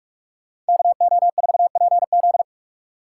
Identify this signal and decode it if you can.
Morse code, heard as a keyed tone, about 59 wpm, 700 Hz. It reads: KO4PZ